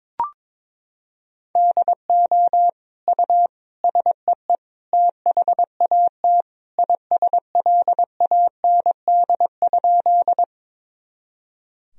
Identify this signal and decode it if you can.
Morse code, heard as a keyed tone, about 22 wpm, 700 Hz. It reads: DO U SEE THAT ISLAND?
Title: Sentences from Top 500 Words and Abbreviations 22wpm